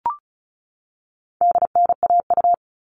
Morse code, heard as a keyed tone, about 34 wpm, 700 Hz. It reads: DNAU
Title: All Cut Numbers - 4-Digit 35wpm